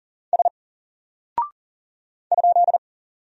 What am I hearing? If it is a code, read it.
Morse code, heard as a keyed tone, about 39 wpm, 700 Hz. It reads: S ?
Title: Single Character (Letter:Number:Punctuation:Prosign) Alt 40wpm